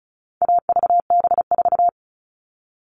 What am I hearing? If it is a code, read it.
Morse code, heard as a keyed tone, about 34 wpm, 700 Hz. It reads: AVB4
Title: All Cut Numbers - 4-Digit 35wpm